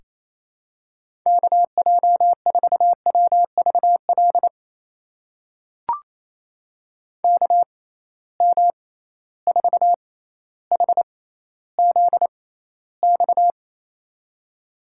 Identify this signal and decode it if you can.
Morse code, heard as a keyed tone, about 28 wpm, spaced as for 14 wpm, 700 Hz. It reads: KJ4WVL KM4HZX
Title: USA Call Signs - 6 Characters - 6x Character Spacing - 28wpm